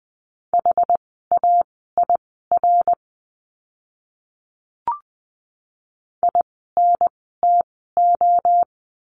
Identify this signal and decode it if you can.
Morse code, heard as a keyed tone, about 20 wpm, spaced as for 15 wpm, 700 Hz. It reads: HAIR INTO
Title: 101 Lesson 17f - Letter u - Words 20wpm